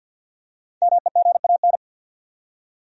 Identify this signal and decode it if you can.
Morse code, heard as a keyed tone, about 50 wpm, 700 Hz. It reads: MEGAN
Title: USA Common Names - Max 5 Letters - 50wpm